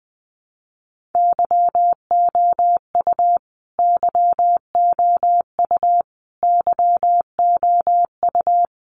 Morse code, heard as a keyed tone, about 20 wpm, 700 Hz. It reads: YOU YOU YOU
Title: Top 500 Words Repeated 20wpm